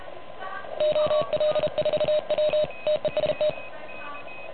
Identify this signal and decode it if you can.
Morse code, heard as a keyed tone, about 33 wpm, 600 Hz. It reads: OL4W TEST
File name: truSDX / MWC29.8